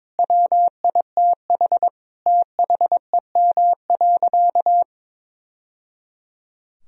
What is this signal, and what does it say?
Morse code, heard as a keyed tone, about 22 wpm, 700 Hz. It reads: WITH THEM.
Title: Sentences from Top 1000 Words - Review Part 3 - 22wpm